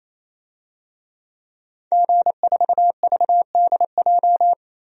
Morse code, heard as a keyed tone, about 28 wpm, 700 Hz. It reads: G4VDJ
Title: DXCC-100 Call Signs - 5 Characters - 28wpm